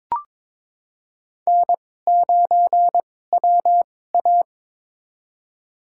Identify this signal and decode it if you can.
Morse code, heard as a keyed tone, about 22 wpm, spaced as for 16 wpm, 700 Hz. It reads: N9WA
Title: USA Call Signs - 4 Characters - 2x Character Spacing - 22wpm